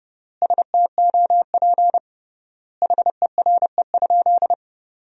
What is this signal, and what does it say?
Morse code, heard as a keyed tone, about 30 wpm, 700 Hz. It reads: STOP HERE?
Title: 3x Word-Spacing - Sentences from Top 300 Words - 30wpm